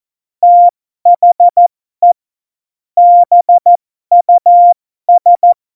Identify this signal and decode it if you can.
Morse code, heard as a keyed tone, about 14 wpm, spaced as for 12 wpm, 700 Hz. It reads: THE BUS
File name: Morsekurs / Witze